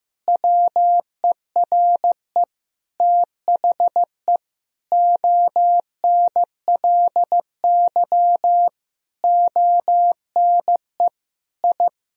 Morse code, heard as a keyed tone, about 15 wpm, 700 Hz. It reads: WERE THE ONLY ONE I
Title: Sentences from Top 1000 Words - Review Part 4 - 15wpm